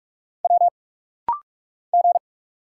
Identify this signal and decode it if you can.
Morse code, heard as a keyed tone, about 45 wpm, 690 Hz. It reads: W G
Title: Single Letters - Rapid-Fire 45wpm